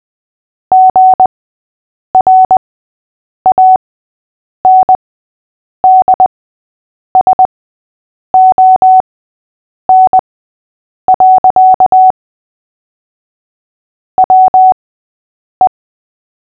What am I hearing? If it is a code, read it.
Morse code, heard as a keyed tone, about 20 wpm, spaced as for 8 wpm, 750 Hz. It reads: GRANDSON. WE
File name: cw / fists / Short-Story-5-8wpm